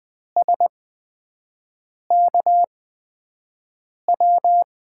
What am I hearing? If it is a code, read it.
Morse code, heard as a keyed tone, about 20 wpm, spaced as for 5 wpm, 700 Hz. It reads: SKW
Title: USA Call Signs - 8x Character Spacing - 20wpm